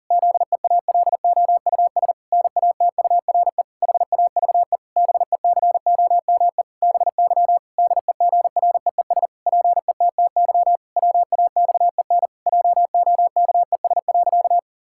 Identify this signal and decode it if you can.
Morse code, heard as a keyed tone, about 40 wpm, 700 Hz. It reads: ZEALOUS NATURE HAVE BECOME BY DEGREES PETTY WAXEN JOKES.
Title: Pangram Challenge v2 - 40wpm